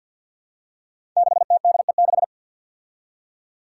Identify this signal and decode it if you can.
Morse code, heard as a keyed tone, about 47 wpm, 700 Hz. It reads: 6TDE6
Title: Five Challenging Letter-Number Combinations 50wpm